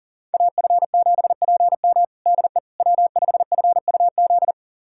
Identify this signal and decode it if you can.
Morse code, heard as a keyed tone, about 40 wpm, 700 Hz. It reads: AF7PM DE W5FUZ
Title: USA Callsign DE Callsign - 5 Characters - 40wpm